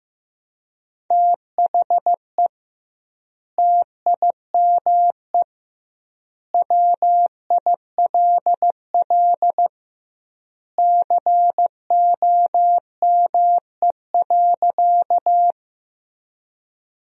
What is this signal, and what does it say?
Morse code, heard as a keyed tone, about 16 wpm, 700 Hz. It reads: THE TIME WILL COME.
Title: 2x Word-Spacing - Sentences from Top 100 Words - 15wpm